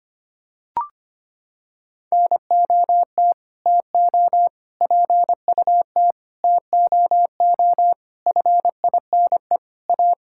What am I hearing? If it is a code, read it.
Morse code, heard as a keyed tone, about 25 wpm, 700 Hz. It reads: NOT TO PUT TOO FINE A
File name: Sets of 9 Words - 25wpm